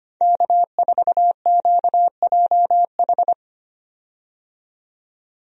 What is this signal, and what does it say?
Morse code, heard as a keyed tone, about 25 wpm, 700 Hz. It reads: K4QJH
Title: USA Call Signs - 5 Characters - 25wpm